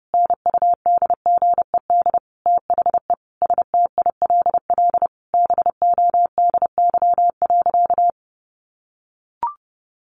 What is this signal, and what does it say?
Morse code, heard as a keyed tone, about 30 wpm, 700 Hz. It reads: NUDGED THE STILL BODY.
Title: A Sound Of Thunder 30wpm